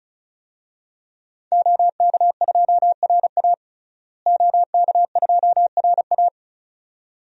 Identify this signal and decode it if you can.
Morse code, heard as a keyed tone, about 36 wpm, 700 Hz. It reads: OK2RA OK2RA
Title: DXCC-100 Call Signs - 5 Characters - Repeated 2x - 35wpm